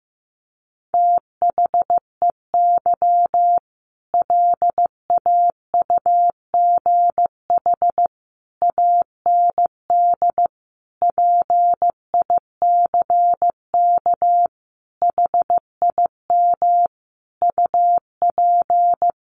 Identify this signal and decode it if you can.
Morse code, heard as a keyed tone, about 15 wpm, 700 Hz. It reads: THEY LAUGH AND PICK HIM UP
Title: Sentences from Top 1000 Words 15wpm